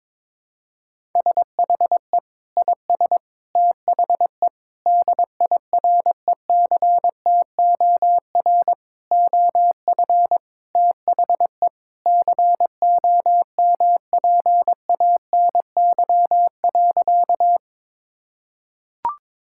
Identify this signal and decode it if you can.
Morse code, heard as a keyed tone, about 22 wpm, 700 Hz. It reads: SHE IS THE DIRECTOR OF THE COMPANY.
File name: practice / 22wpm / Sentences with 3-Word Phrases - 22wpm